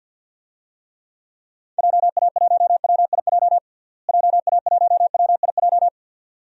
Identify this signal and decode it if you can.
Morse code, heard as a keyed tone, about 52 wpm, 700 Hz. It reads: JA1WIJ JA1WIJ
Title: DXCC-100 Call Signs - 6 Characters - Repeated 2x - 50wpm